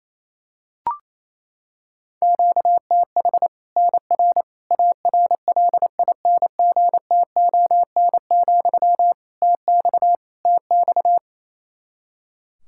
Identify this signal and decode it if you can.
Morse code, heard as a keyed tone, about 28 wpm, 700 Hz. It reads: QTH NR ARLINGTON, TX TX
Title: K1USN SST QSO Practice 28wpm